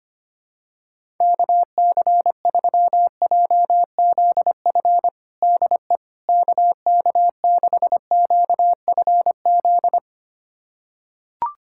Morse code, heard as a keyed tone, about 25 wpm, 700 Hz. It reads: KC3JZF DE KK6QFZ
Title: USA Callsign DE Callsign 25wpm